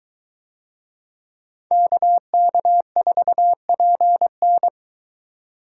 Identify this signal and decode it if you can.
Morse code, heard as a keyed tone, about 23 wpm, 700 Hz. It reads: KK4PN